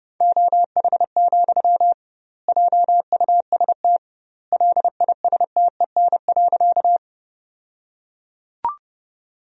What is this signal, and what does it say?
Morse code, heard as a keyed tone, about 30 wpm, 700 Hz. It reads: OH, JUST LISTEN.
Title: 2x Word-Spacing - Sentences from Top 500 Words - 30wpm